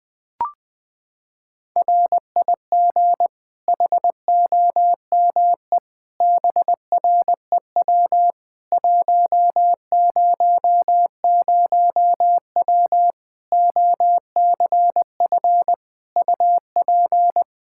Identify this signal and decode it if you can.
Morse code, heard as a keyed tone, about 21 wpm, 700 Hz. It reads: RIG HOME BREW 100W OCF UP
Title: K1USN SST QSO Practice 20wpm